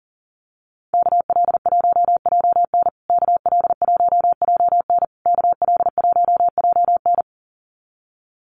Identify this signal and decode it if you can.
Morse code, heard as a keyed tone, about 40 wpm, 700 Hz. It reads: KL1JN KL1JN KL1JN
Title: USA Call Signs Repeated 3x 40wpm